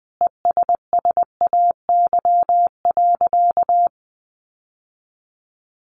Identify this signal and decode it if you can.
Morse code, heard as a keyed tone, about 20 wpm, 700 Hz. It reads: ESSAY.
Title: Sentences from Top 1500 Words 20wpm